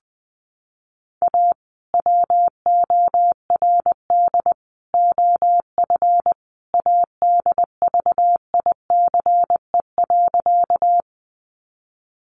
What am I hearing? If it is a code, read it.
Morse code, heard as a keyed tone, about 20 wpm, 700 Hz. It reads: A WORD OF ADVICE.
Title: Sentences from Top 1100 Words 20wpm